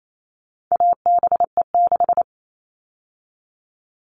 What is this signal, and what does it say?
Morse code, heard as a keyed tone, about 28 wpm, 700 Hz. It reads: ABE6
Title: All Cut Numbers - 4-Digit 28wpm